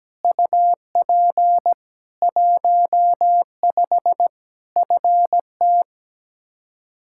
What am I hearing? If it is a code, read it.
Morse code, heard as a keyed tone, about 17 wpm, 700 Hz. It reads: UP 15 FT
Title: Practice QSO - Exchange 2 - Info - 17wpm